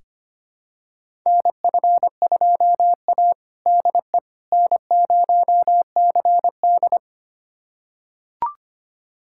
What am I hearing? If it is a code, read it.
Morse code, heard as a keyed tone, about 25 wpm, 700 Hz. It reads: NF2A DE N0CD
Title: USA Callsign DE Callsign - 4 Characters - 25wpm